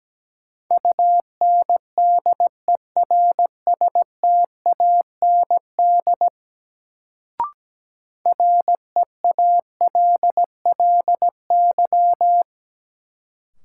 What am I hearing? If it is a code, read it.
Morse code, heard as a keyed tone, about 17 wpm, 700 Hz. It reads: UNDERSTAND REALLY
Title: Sentences from Top 600 Words - New Word Drill - 17wpm